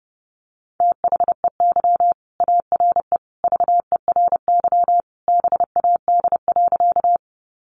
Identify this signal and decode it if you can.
Morse code, heard as a keyed tone, about 30 wpm, 700 Hz. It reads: THEY ARE VERY BAD.
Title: Sentences from Top 700 Words 30wpm